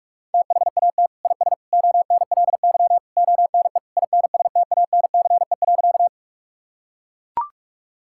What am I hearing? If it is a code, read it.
Morse code, heard as a keyed tone, about 41 wpm, 700 Hz. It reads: THAT IS ONLY ONE INSTANCE.
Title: Sentences from Top 1000 Words - Review Part 5 - 45wpm